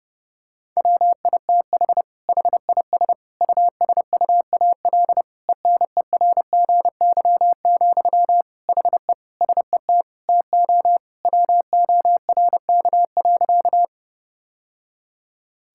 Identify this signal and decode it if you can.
Morse code, heard as a keyed tone, about 30 wpm, 700 Hz. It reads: WITH HIS USUAL ENERGY, HE SET TO WORK.
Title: Sentences from Top 1000 Words - Review Part 6 - 30wpm